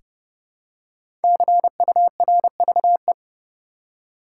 Morse code, heard as a keyed tone, about 29 wpm, 700 Hz. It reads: CURVE
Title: Five Letter Words - 30wpm